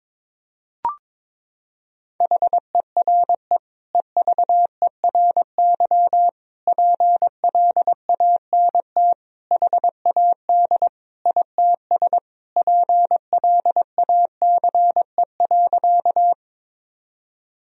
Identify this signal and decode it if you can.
Morse code, heard as a keyed tone, about 21 wpm, 700 Hz. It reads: HERE EVERY PLANT HAD ITS PLACE.
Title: Sentences from Top 1000 Words - Review Part 1 - 22wpm